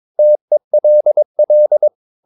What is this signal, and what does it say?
Morse code, heard as a keyed tone, about 22 wpm, 600 Hz. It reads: TELL
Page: Romanen Never af Ken Follet på engelsk.